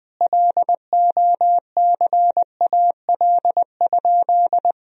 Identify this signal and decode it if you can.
Morse code, heard as a keyed tone, about 20 wpm, 700 Hz. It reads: LOCAL?
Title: Sentences from Top 1000 Words - Review Part 3 - 20wpm